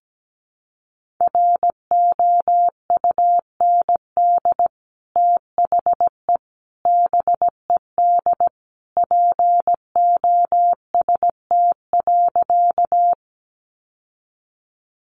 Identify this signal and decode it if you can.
Morse code, heard as a keyed tone, about 17 wpm, 700 Hz. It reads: ROUND THE BED POST.
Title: Sentences from Top 1000 Words 17wpm